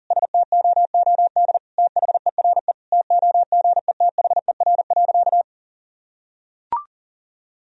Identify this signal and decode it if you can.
Morse code, heard as a keyed tone, about 40 wpm, 700 Hz. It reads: STOOD THERE TOGETHER.
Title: Sentences from Top 1000 Words - Review Part 2 - 40wpm